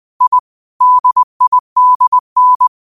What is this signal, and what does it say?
Morse code, heard as a keyed tone, about 20 wpm, 1000 Hz. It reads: I DIDN
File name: qotdfiles / 20wpm